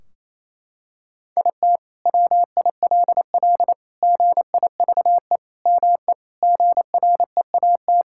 Morse code, heard as a keyed tone, about 28 wpm, 700 Hz. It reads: IT WILL GIVE ME GREAT
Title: Sentences from Top 1600 Words 28wpm